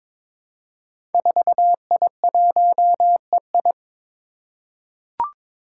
Morse code, heard as a keyed tone, about 22 wpm, 700 Hz. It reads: 4I1EI
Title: Five Challenging Letter-Number Combinations 22wpm